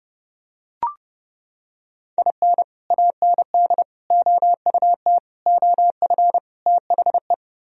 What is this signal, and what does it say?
Morse code, heard as a keyed tone, about 32 wpm, 700 Hz. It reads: IN AND OUT OF THE